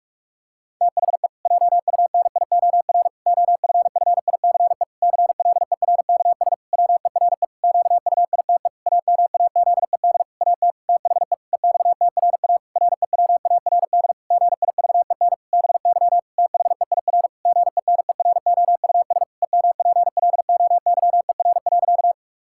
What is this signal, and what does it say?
Morse code, heard as a keyed tone, about 41 wpm, 700 Hz. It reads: THE JUNIOR OFFICE CLERKS WERE QUITE AMAZED AT THE EXTRA REWARD GIVEN BY THEIR GENEROUS EMPLOYER.
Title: Pangram Challenge v2 - 45wpm